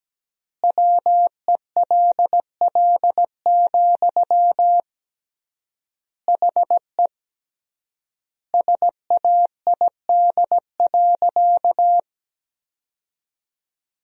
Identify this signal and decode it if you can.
Morse code, heard as a keyed tone, about 17 wpm, 700 Hz. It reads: WELL, HE SAID.
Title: 3x Word-Spacing - Sentences from Top 200 Words - 17wpm